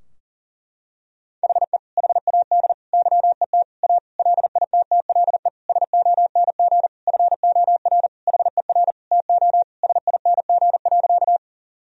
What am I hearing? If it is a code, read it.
Morse code, heard as a keyed tone, about 37 wpm, 700 Hz. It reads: HE HAD YET A LITTLE SONG FOR HER TO SING.
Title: Sentences from Top 1000 Words - Review Part 2 - 40wpm